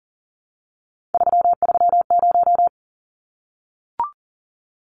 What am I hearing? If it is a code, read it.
Morse code, heard as a keyed tone, about 40 wpm, 700 Hz. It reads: 330
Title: Numbers - 3-Digit 40wpm